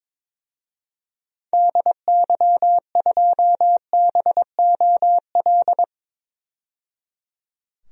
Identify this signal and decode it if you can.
Morse code, heard as a keyed tone, about 22 wpm, 700 Hz. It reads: DY2BOL
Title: DXCC-100 Call Signs - 6 Characters - 22wpm